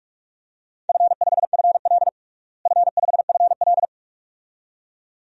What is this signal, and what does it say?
Morse code, heard as a keyed tone, about 46 wpm, 700 Hz. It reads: F5FL F5FL
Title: DXCC-100 Call Signs - 4 Characters - Repeated 2x - 45wpm